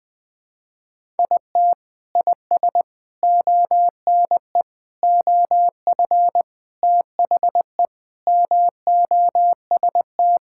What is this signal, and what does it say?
Morse code, heard as a keyed tone, about 20 wpm, 700 Hz. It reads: IT IS ONE OF THE MOST
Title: Sentences with 3-Word Phrases - 20wpm